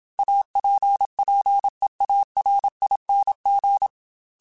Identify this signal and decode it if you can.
Morse code, heard as a keyed tone, about 27 wpm, 800 Hz. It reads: APPEARING
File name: Technical Subjects / Technical Subjects-mp3s / XC @25